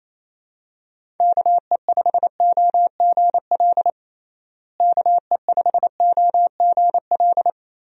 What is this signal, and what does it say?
Morse code, heard as a keyed tone, about 28 wpm, 700 Hz. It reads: KE5OGL KE5OGL
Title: USA Call Signs - 6 Characters - Repeated 2x - 28wpm